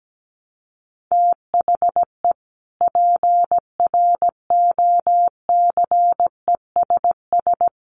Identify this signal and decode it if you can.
Morse code, heard as a keyed tone, about 17 wpm, 700 Hz. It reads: THE PROCESS
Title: Sets of 3 Words 17wpm